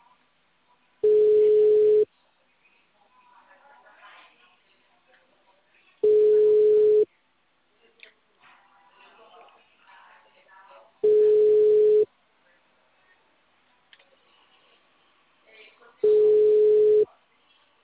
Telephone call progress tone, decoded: ringback tone

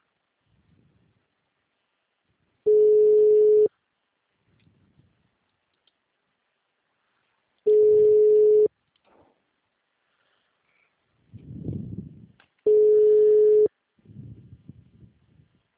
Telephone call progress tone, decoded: ringback tone

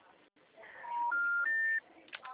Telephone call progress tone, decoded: special information tone

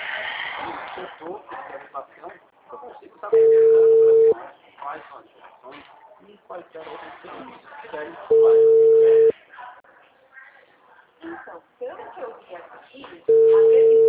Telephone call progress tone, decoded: ringback tone

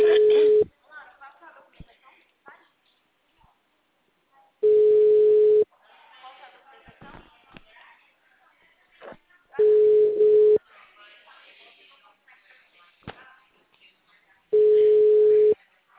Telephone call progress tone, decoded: ringback tone

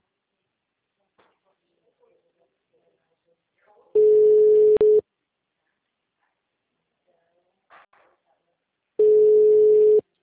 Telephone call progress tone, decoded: ringback tone

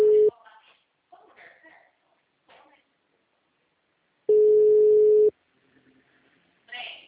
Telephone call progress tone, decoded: ringback tone

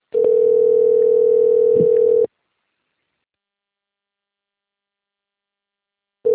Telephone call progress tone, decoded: ringback tone

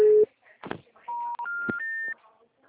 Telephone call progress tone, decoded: special information tone

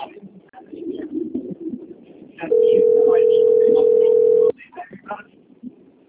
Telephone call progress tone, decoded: ringback tone

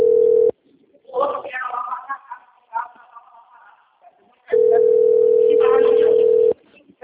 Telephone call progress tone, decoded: ringback tone

